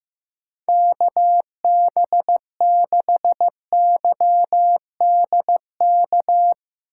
Morse code, heard as a keyed tone, about 15 wpm, 700 Hz. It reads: KB6YDK
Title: USA Call Signs - 6 Characters - 4x Character Spacing - 15wpm